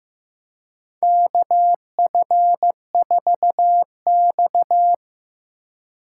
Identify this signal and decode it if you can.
Morse code, heard as a keyed tone, about 15 wpm, 700 Hz. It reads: KF4X
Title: USA Call Signs - 4 Characters - 3x Character Spacing - 15wpm